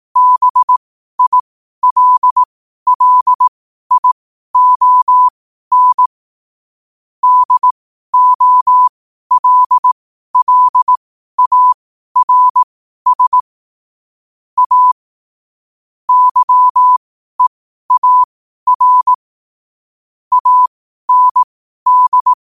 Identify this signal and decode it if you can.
Morse code, heard as a keyed tone, about 18 wpm, spaced as for 12 wpm, 1000 Hz. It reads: BILLION DOLLARS A YEAR AND
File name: qotdfiles / 12wpm